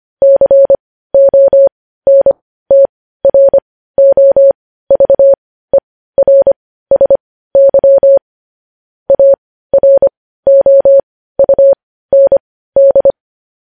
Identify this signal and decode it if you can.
Morse code, heard as a keyed tone, about 25 wpm, spaced as for 15 wpm, 550 Hz. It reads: CONTROVERSY AROUND
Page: news headlines in morse code at 15 wpm